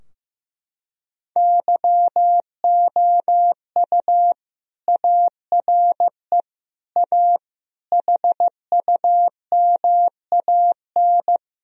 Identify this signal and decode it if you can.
Morse code, heard as a keyed tone, about 15 wpm, 700 Hz. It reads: YOU ARE A HUMAN
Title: Sentences from Top 1000 Words - Review Part 3 - 15wpm